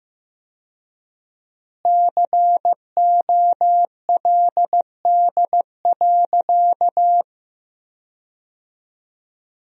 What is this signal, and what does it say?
Morse code, heard as a keyed tone, about 16 wpm, 700 Hz. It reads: COLD.
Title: 4x Word-Spacing - Sentences from Top 400 Words - 15wpm